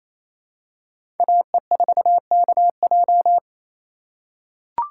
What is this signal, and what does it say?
Morse code, heard as a keyed tone, about 28 wpm, 700 Hz. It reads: AE4KJ
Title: USA Call Signs - 5 Characters - 4x Character Spacing - 28wpm